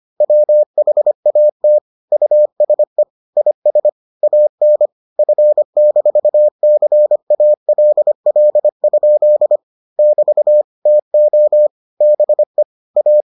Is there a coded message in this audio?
Yes: Morse code from 0.2 to 13.2 s, heard as a keyed tone, about 25 wpm, 600 Hz. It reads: WHAT USE IS AN F-CALL? <BT> TO BE A